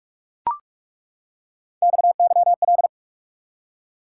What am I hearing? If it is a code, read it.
Morse code, heard as a keyed tone, about 42 wpm, 700 Hz. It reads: XYL
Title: Top 90 QSO Elements 45wpm